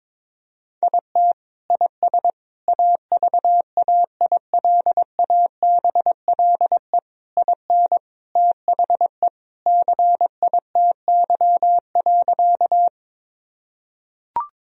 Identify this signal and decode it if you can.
Morse code, heard as a keyed tone, about 22 wpm, 700 Hz. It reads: IT IS AVAILABLE IN THE CITY.
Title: Sentences from Top 1000 Words - Review Part 3 - 22wpm